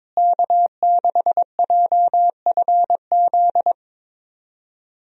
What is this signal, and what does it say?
Morse code, heard as a keyed tone, about 22 wpm, 700 Hz. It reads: K6JFZ
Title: USA Call Signs - 5 Characters - Repeated 2x - 22wpm